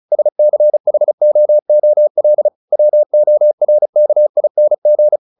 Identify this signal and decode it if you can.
Morse code, heard as a keyed tone, about 34 wpm, 600 Hz. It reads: SCHOOL WORKING
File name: the old man and the sea / the old man and the sea 1 35wpm